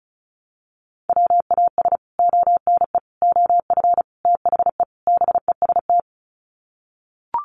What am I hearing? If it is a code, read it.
Morse code, heard as a keyed tone, about 34 wpm, 700 Hz. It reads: WAS ONE OF THE BEST
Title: Sets of 5 Words Encore2 35wpm